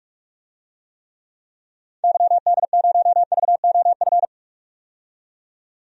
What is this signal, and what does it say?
Morse code, heard as a keyed tone, about 46 wpm, 700 Hz. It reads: YD0VOF
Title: DXCC-100 Call Signs - 6 Characters - 45wpm